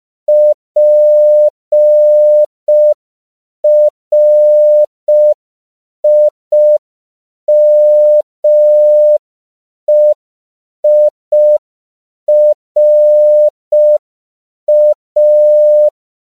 Morse code, heard as a keyed tone, about 5 wpm, 600 Hz. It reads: PRIMEIRA
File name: CW / 5PPM